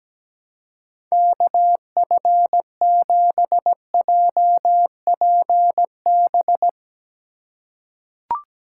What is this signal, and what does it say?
Morse code, heard as a keyed tone, about 17 wpm, 700 Hz. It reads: KF7JPB
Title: USA Call Signs - 6 Characters - 3x Character Spacing - 17wpm